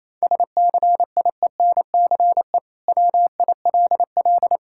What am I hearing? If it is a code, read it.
Morse code, heard as a keyed tone, about 28 wpm, 700 Hz. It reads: SCIENCE WILL
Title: Sentences from Top 1000 Words - Review Part 6 - 28wpm